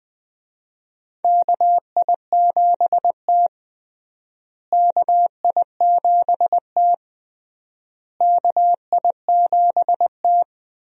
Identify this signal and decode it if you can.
Morse code, heard as a keyed tone, about 20 wpm, 700 Hz. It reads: KI7T KI7T KI7T
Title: USA Call Signs - 4 Characters - Repeated 3x - 20wpm